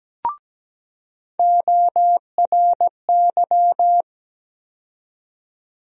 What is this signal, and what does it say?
Morse code, heard as a keyed tone, about 17 wpm, 700 Hz. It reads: ORY
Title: USA Call Sign Postfixes - 3 Letters 17wpm